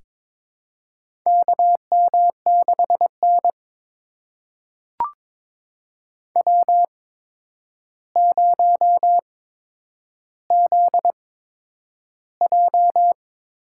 Morse code, heard as a keyed tone, about 22 wpm, spaced as for 10 wpm, 700 Hz. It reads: KM6N W0ZJ
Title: USA Call Signs - 4 Characters - 8x Character Spacing - 22wpm